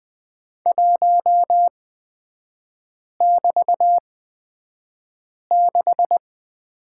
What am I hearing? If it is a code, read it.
Morse code, heard as a keyed tone, about 20 wpm, spaced as for 5 wpm, 700 Hz. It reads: 1<BT>6
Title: 224 Lesson 36e - BT - Three Characters 20wpm